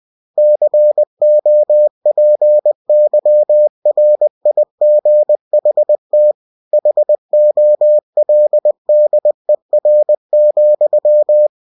Morse code, heard as a keyed tone, about 20 wpm, 600 Hz. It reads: COPYRIGHT HOLDER,
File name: cw-books / winnie-the-pooh / winnie-20wpm-page-0115